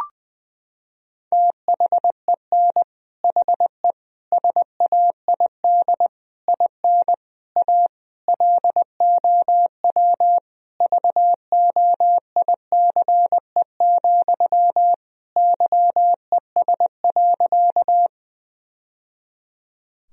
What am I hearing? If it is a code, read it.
Morse code, heard as a keyed tone, about 21 wpm, 700 Hz. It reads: THEN HE SAID IN A LOW VOICE, YES.
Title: Sentences from Top 1000 Words - Review Part 1 - 20wpm